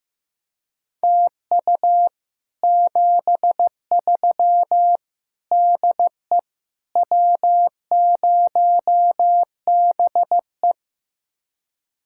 Morse code, heard as a keyed tone, about 16 wpm, 700 Hz. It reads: TU 73 DE W0BE
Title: Speed-Racing Common QSO Phrases Encore - 15wpm